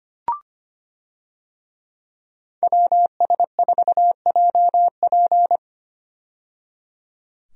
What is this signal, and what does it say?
Morse code, heard as a keyed tone, about 25 wpm, 700 Hz. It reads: WS4JP
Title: USA Call Signs - 5 Characters - 25wpm